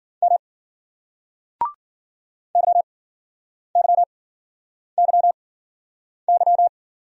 Morse code, heard as a keyed tone, about 54 wpm, spaced as for 9 wpm, 700 Hz. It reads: M YYYY